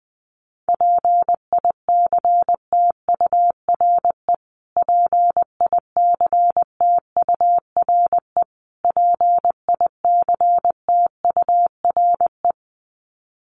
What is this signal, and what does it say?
Morse code, heard as a keyed tone, about 20 wpm, 700 Hz. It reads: PICTURE PICTURE PICTURE
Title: Top 200 Words Repeated 20wpm